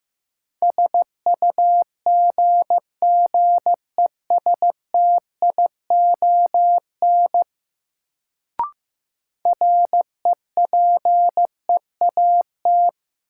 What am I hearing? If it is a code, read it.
Morse code, heard as a keyed tone, about 15 wpm, 700 Hz. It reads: SUGGESTION REPEAT
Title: Sentences from Top 1600 Words - New Word Drill - 15wpm